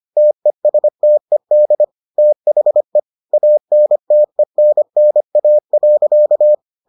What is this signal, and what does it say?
Morse code, heard as a keyed tone, about 25 wpm, 600 Hz. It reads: TESTED THE ANTENNA.